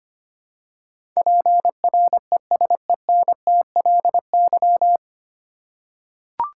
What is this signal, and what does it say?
Morse code, heard as a keyed tone, about 25 wpm, 700 Hz. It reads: PRESENTLY
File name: Nine Letter Words - 25wpm